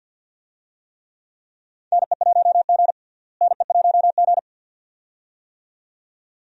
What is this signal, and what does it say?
Morse code, heard as a keyed tone, about 50 wpm, 700 Hz. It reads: NE1G NE1G
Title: USA Call Signs - 4 Characters - Repeated 2x - 50wpm